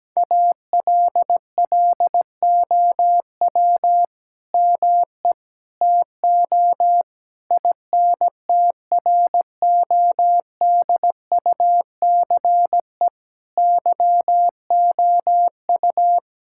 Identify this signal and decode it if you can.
Morse code, heard as a keyed tone, about 17 wpm, 700 Hz. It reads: ALLOW ME TO INTRODUCE YOU
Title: Sentences from Top 1600 Words 17wpm